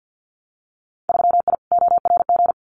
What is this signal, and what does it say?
Morse code, heard as a keyed tone, about 50 wpm, 700 Hz. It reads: 3I ORG
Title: Winter Field Day Exchanges Encore 50wpm